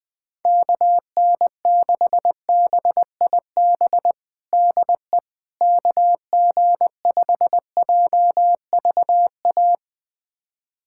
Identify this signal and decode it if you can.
Morse code, heard as a keyed tone, about 21 wpm, 700 Hz. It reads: KN6BIB DE KG5JVA
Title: USA Callsign DE Callsign - 6 Characters - 20wpm